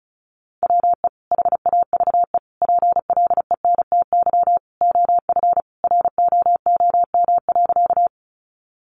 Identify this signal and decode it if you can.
Morse code, heard as a keyed tone, about 34 wpm, 700 Hz. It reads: WE HAVE PLENTY OF ROOM.
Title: Sentences from Top 1300 Words 35wpm